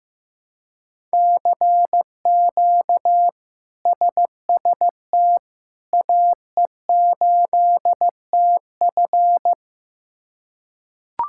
Morse code, heard as a keyed tone, about 15 wpm, 700 Hz. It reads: CQ SST AE8TF
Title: K1USN SST QSO Practice - 15wpm